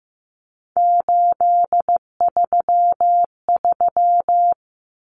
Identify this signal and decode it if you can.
Morse code, heard as a keyed tone, about 15 wpm, 700 Hz. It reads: 833
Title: Numbers - 3-Digit 15wpm